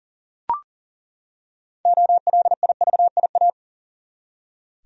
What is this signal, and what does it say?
Morse code, heard as a keyed tone, about 40 wpm, 700 Hz. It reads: OLIVIA
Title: USA Common Names - 40wpm